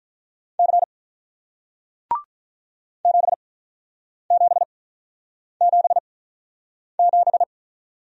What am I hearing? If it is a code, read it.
Morse code, heard as a keyed tone, about 48 wpm, spaced as for 9 wpm, 700 Hz. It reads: C 7777